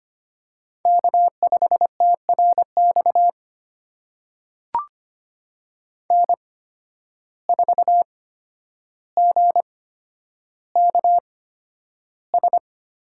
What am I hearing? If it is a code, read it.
Morse code, heard as a keyed tone, about 25 wpm, spaced as for 11 wpm, 700 Hz. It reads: K5TRX N4GKS
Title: USA Call Signs - 5 Characters - 8x Character Spacing - 25wpm